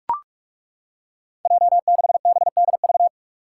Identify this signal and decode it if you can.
Morse code, heard as a keyed tone, about 43 wpm, 700 Hz. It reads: J6BDV
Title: Five Challenging Letter-Number Combinations 45wpm